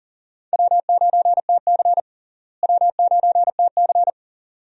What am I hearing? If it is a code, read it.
Morse code, heard as a keyed tone, about 40 wpm, 700 Hz. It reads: W9TC W9TC
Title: USA Call Signs - 4 Characters - Repeated 2x - 40wpm